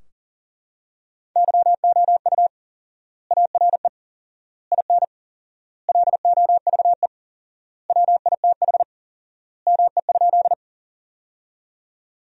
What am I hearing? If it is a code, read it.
Morse code, heard as a keyed tone, about 40 wpm, 700 Hz. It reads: YOU ARE IN LOVE WITH ME?
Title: 4x Word-Spacing - Sentences from Top 400 Words - 40wpm